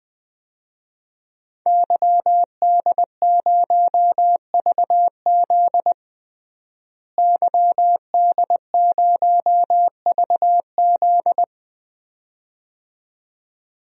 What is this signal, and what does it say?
Morse code, heard as a keyed tone, about 20 wpm, 700 Hz. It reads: YD0VZ YD0VZ
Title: DXCC-100 Call Signs - 5 Characters - Repeated 2x - 20wpm